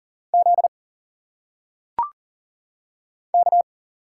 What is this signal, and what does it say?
Morse code, heard as a keyed tone, about 40 wpm, 700 Hz. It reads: Z K
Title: Single Letters - Phonetic Pronunciation 40wpm